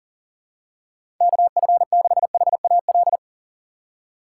Morse code, heard as a keyed tone, about 40 wpm, 700 Hz. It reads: KF6HAL
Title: USA Call Signs - 6 Characters - 3x Character Spacing - 40wpm